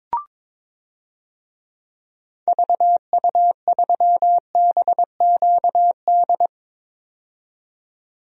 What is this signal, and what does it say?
Morse code, heard as a keyed tone, about 22 wpm, 700 Hz. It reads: VU3BQD
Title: DXCC-100 Call Signs - 6 Characters - 22wpm